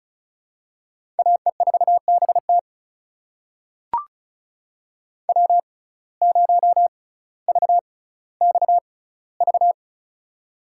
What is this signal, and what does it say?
Morse code, heard as a keyed tone, about 36 wpm, spaced as for 18 wpm, 700 Hz. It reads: AE4BT W0VXV
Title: USA Call Signs - 5 Characters - 6x Character Spacing - 35wpm